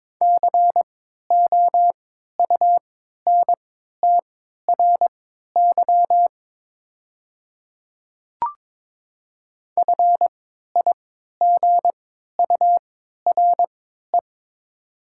Morse code, heard as a keyed tone, about 22 wpm, spaced as for 12 wpm, 700 Hz. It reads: COUNTRY FIGURE
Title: 3x Character Spacing - Top 500 Words - 22wpm